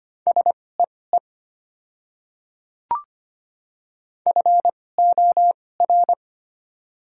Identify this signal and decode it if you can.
Morse code, heard as a keyed tone, about 25 wpm, spaced as for 18 wpm, 690 Hz. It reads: SEE FOR
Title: 2x Character Spacing - Top 100 Words - 25wpm